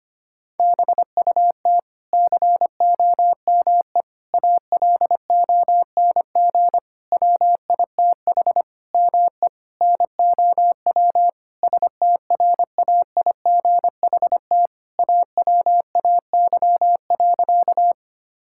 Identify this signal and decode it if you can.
Morse code, heard as a keyed tone, about 25 wpm, 700 Hz. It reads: BUT COME ALONG WITH ME NOW STRAIGHT AWAY.
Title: Sentences from Top 1000 Words - Review Part 6 - 25wpm